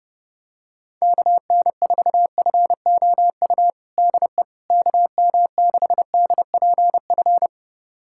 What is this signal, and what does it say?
Morse code, heard as a keyed tone, about 30 wpm, 700 Hz. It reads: KN4FOU DE KM6DPF
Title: USA Callsign DE Callsign - 6 Characters - 30wpm